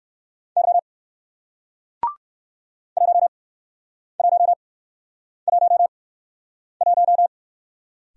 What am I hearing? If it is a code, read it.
Morse code, heard as a keyed tone, about 58 wpm, spaced as for 9 wpm, 700 Hz. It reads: Y 1111